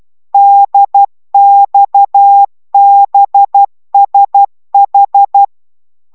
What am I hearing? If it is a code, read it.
Morse code, heard as a keyed tone, about 12 wpm, 800 Hz. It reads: DXBSH
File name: taviro tanulas / cw 69 16 10